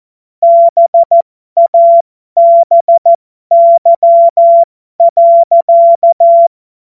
Morse code, heard as a keyed tone, about 14 wpm, spaced as for 12 wpm, 680 Hz. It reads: BABY.
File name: Morsekurs / Witze